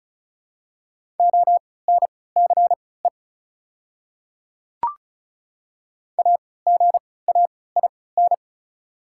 Morse code, heard as a keyed tone, about 36 wpm, spaced as for 20 wpm, 690 Hz. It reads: ONCE AGAIN
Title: 3x Character Spacing - Top 300 Words - 35wpm